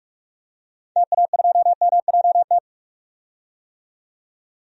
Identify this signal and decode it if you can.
Morse code, heard as a keyed tone, about 45 wpm, 700 Hz. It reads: TA2MJT
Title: DXCC-100 Call Signs - 6 Characters - 45wpm